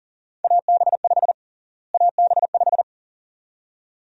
Morse code, heard as a keyed tone, about 37 wpm, 700 Hz. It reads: AB5 AB5
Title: USA Call Sign Prefixes - Repeated 2x - 40wpm